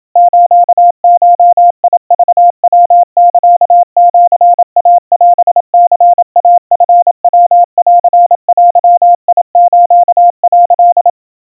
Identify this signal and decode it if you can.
Morse code (Japanese Wabun code), heard as a keyed tone, about 27 wpm, 700 Hz. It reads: スゴクヤサシイオニイチヤンデス」
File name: kawa / Morse sample / kamyu 100 cpm 700 Hz Crispy